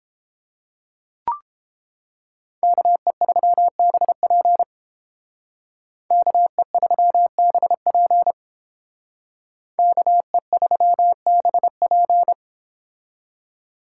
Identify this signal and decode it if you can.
Morse code, heard as a keyed tone, about 28 wpm, 700 Hz. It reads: KE3BP KE3BP KE3BP